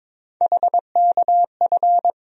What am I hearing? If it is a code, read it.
Morse code, heard as a keyed tone, about 22 wpm, 700 Hz. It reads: HKF
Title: USA Call Sign Postfixes - 3 Letters 22wpm